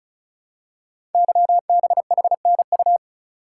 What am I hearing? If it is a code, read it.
Morse code, heard as a keyed tone, about 36 wpm, 700 Hz. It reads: YBHNU
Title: Code Groups - 5 Letter:Numbers - 35wpm